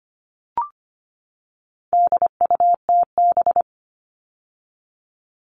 Morse code, heard as a keyed tone, about 25 wpm, 700 Hz. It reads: DUTB
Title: All Cut Numbers - 4-Digit 25wpm